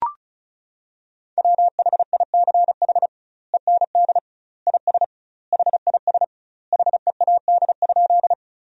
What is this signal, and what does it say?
Morse code, heard as a keyed tone, about 36 wpm, 700 Hz. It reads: WHICH END IS HIS HEAD?
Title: 2x Word-Spacing - Sentences from Top 300 Words - 35wpm